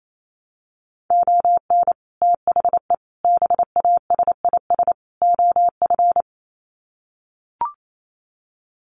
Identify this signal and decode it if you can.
Morse code, heard as a keyed tone, about 28 wpm, 700 Hz. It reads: ON THE BASIS OF
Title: Sets of 4 Words 28wpm